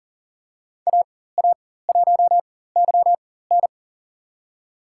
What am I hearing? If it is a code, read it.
Morse code, heard as a keyed tone, about 38 wpm, spaced as for 18 wpm, 700 Hz. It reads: AA1YN
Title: USA Call Signs - 5 Characters - 4x Character Spacing - 40wpm